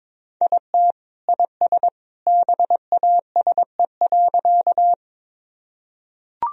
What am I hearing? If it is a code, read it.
Morse code, heard as a keyed tone, about 22 wpm, 700 Hz. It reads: IT IS BASE.
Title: Sentences from Top 1000 Words - Review Part 1 - 22wpm